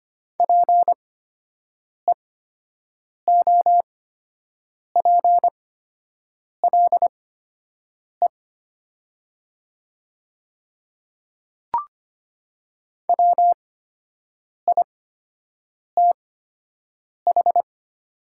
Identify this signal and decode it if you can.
Morse code, heard as a keyed tone, about 25 wpm, spaced as for 7 wpm, 700 Hz. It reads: PEOPLE WITH
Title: 8x Character Spacing - Top 100 Words - 25wpm